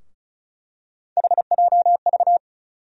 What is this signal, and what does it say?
Morse code, heard as a keyed tone, about 36 wpm, 700 Hz. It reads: HJV
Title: USA Call Sign Postfixes - 3 Letters 35wpm